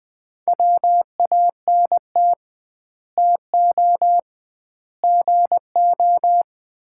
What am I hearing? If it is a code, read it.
Morse code, heard as a keyed tone, about 20 wpm, 700 Hz. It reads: WANT TO GO
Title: 2x Word-Spacing - Sets of 3 Words - 20wpm